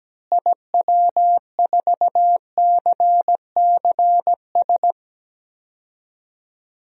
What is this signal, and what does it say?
Morse code, heard as a keyed tone, about 17 wpm, 700 Hz. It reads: IW4CCS
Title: DXCC-100 Call Signs - 6 Characters - Repeated 3x - 17wpm